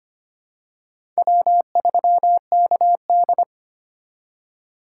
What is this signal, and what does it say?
Morse code, heard as a keyed tone, about 25 wpm, 700 Hz. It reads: W3KD
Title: USA Call Signs - 4 Characters - 2x Character Spacing - 25wpm